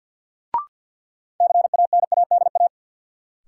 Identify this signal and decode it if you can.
Morse code, heard as a keyed tone, about 52 wpm, 700 Hz. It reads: CANADA
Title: Sets of 5 Words - New Word Drill - 50wpm